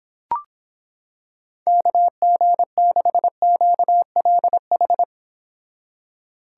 Morse code, heard as a keyed tone, about 26 wpm, 700 Hz. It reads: KG6QLH